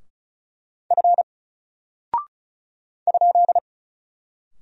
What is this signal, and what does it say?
Morse code, heard as a keyed tone, about 36 wpm, 700 Hz. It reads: F ?
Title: Single Character (Letter:Number:Punctuation:Prosign) 35wpm